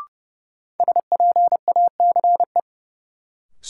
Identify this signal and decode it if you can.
Morse code, heard as a keyed tone, about 30 wpm, 700 Hz. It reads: SPACE
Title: Sentences from Top 1500 Words - New Word Drill Alt - 30wpm